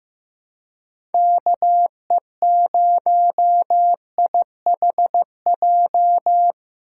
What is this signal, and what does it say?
Morse code, heard as a keyed tone, about 15 wpm, 700 Hz. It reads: KE0IHJ
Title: USA Call Signs - 6 Characters - 3x Character Spacing - 15wpm